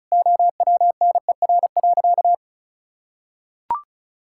Morse code, heard as a keyed tone, about 36 wpm, 700 Hz. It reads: OWNER.
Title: Sentences from Top 1600 Words 35wpm